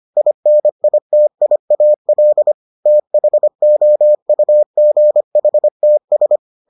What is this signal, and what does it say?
Morse code, heard as a keyed tone, about 25 wpm, 600 Hz. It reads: INITIAL THOUGHTS